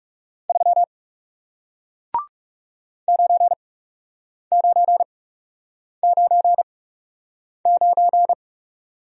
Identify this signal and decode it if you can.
Morse code, heard as a keyed tone, about 38 wpm, spaced as for 8 wpm, 700 Hz. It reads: 3 9999